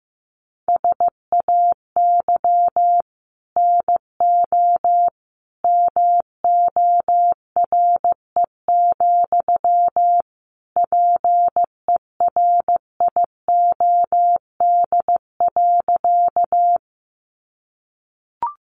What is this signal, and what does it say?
Morse code, heard as a keyed tone, about 15 wpm, 700 Hz. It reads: SAY NO MORE, PERIOD.
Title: Sentences from Top 800 Words 15wpm